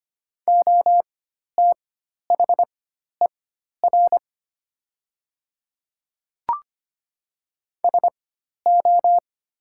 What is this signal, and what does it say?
Morse code, heard as a keyed tone, about 25 wpm, spaced as for 12 wpm, 700 Hz. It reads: OTHER SO
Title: 4x Character Spacing - Top 100 Words - 25wpm